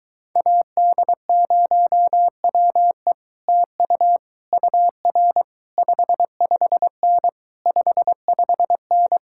Morse code, heard as a keyed tone, about 23 wpm, 700 Hz. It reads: AD0WE TU UR 55N 55N